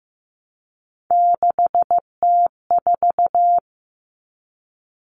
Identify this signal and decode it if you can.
Morse code, heard as a keyed tone, about 15 wpm, 700 Hz. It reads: 6T4
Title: Common Cut Numbers - 3-Digit 15wpm